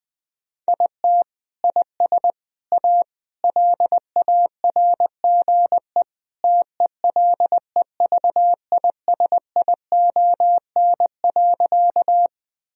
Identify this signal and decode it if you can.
Morse code, heard as a keyed tone, about 20 wpm, 700 Hz. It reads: IT IS A LARGE TELEVISION.
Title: Sentences from Top 1000 Words - Review Part 4 - 20wpm